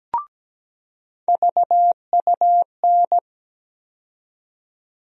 Morse code, heard as a keyed tone, about 17 wpm, 700 Hz. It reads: VUN
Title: USA Call Sign Postfixes - 3 Letters 17wpm